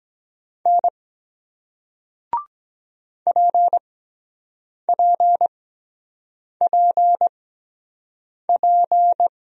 Morse code, heard as a keyed tone, about 23 wpm, spaced as for 7 wpm, 700 Hz. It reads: N PPPP